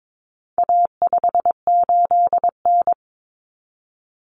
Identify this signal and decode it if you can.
Morse code, heard as a keyed tone, about 22 wpm, 700 Hz. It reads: A58N
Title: Common Cut Numbers - 4-Digit 22wpm